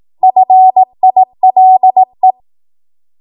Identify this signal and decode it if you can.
Morse code, heard as a keyed tone, about 18 wpm, 750 Hz. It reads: FILE
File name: Morse / 18WPM